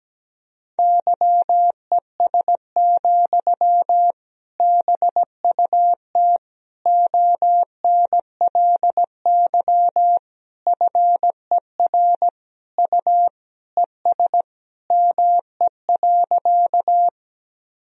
Morse code, heard as a keyed tone, about 17 wpm, 700 Hz. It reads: YES, BUT ONLY FER U ES ME.
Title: Sentences from Top 500 Words and Abbreviations 17wpm